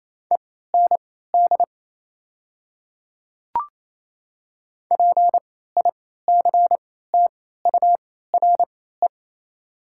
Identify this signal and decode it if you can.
Morse code, heard as a keyed tone, about 28 wpm, spaced as for 16 wpm, 690 Hz. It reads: END PICTURE
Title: 3x Character Spacing - Top 500 Words - 28wpm